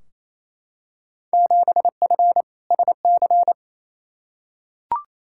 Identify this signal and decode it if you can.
Morse code, heard as a keyed tone, about 28 wpm, 700 Hz. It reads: 7F SC
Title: Field Day Exchanges Encore - 28wpm